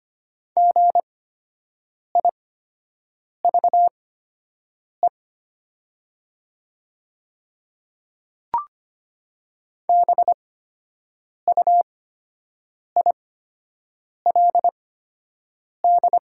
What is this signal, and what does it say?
Morse code, heard as a keyed tone, about 25 wpm, spaced as for 7 wpm, 700 Hz. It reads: GIVE BUILD
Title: 8x Character Spacing - Top 300 Words - 25wpm